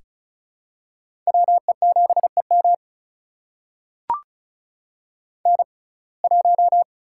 Morse code, heard as a keyed tone, about 36 wpm, spaced as for 25 wpm, 700 Hz. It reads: WE7EM N1
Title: USA Call Signs - 5 Characters - 6x Character Spacing - 35wpm